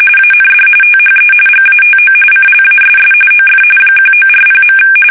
RTTY, about 45.45 baud. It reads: AND THE GATO CLASS OF 1940.
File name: fsk850